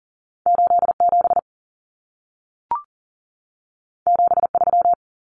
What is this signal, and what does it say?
Morse code, heard as a keyed tone, about 40 wpm, 700 Hz. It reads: 87 73
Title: Common Cut Numbers - 2-Digit 40wpm